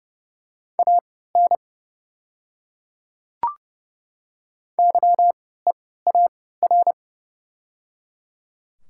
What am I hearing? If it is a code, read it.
Morse code, heard as a keyed tone, about 30 wpm, spaced as for 17 wpm, 700 Hz. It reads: AN YEAR
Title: 3x Character Spacing - Top 200 Words - 30wpm